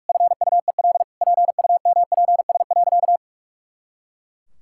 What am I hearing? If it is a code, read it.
Morse code, heard as a keyed tone, about 41 wpm, 700 Hz. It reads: FUEL PUMPS.
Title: Pangram Challenge v2 - 45wpm